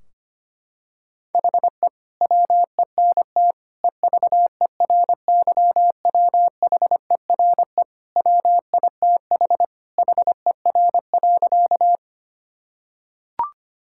Morse code, heard as a keyed tone, about 25 wpm, 700 Hz. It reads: HE WENT EVERYWHERE WITH HER.
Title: Sentences from Top 1600 Words 25wpm